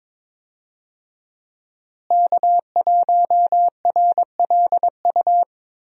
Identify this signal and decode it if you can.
Morse code, heard as a keyed tone, about 22 wpm, 700 Hz. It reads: K1RLU
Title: USA Call Signs - 5 Characters - Repeated 2x - 22wpm